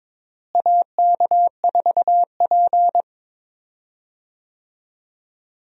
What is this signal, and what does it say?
Morse code, heard as a keyed tone, about 22 wpm, 700 Hz. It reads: AK4P
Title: USA Call Signs - 4 Characters - Repeated 2x - 22wpm